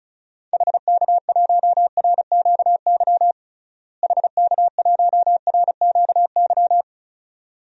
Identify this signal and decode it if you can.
Morse code, heard as a keyed tone, about 34 wpm, 700 Hz. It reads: HK1RQY HK1RQY
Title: DXCC-100 Call Signs - 6 Characters - Repeated 2x - 35wpm